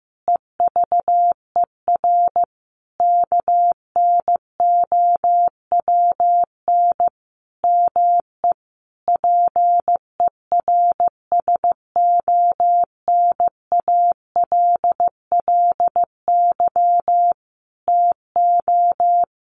Morse code, heard as a keyed tone, about 15 wpm, 700 Hz. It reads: EVER KNOWN ME PERSONALLY TO
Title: Sentences from Top 1400 Words 15wpm